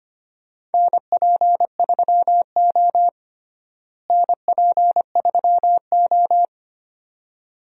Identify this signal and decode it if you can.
Morse code, heard as a keyed tone, about 25 wpm, 700 Hz. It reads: NP3O NP3O
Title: USA Call Signs - 4 Characters - Repeated 2x - 25wpm